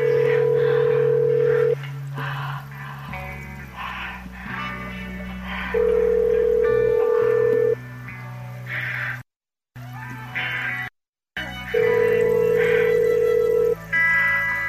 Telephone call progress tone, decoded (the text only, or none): ringback tone